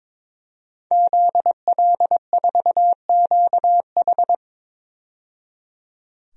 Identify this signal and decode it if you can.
Morse code, heard as a keyed tone, about 21 wpm, 700 Hz. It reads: ZL4QH
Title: DXCC-100 Call Signs - 5 Characters - 22wpm